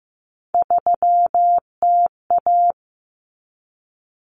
Morse code, heard as a keyed tone, about 15 wpm, 700 Hz. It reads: 3TA
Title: Common Cut Numbers - 3-Digit 15wpm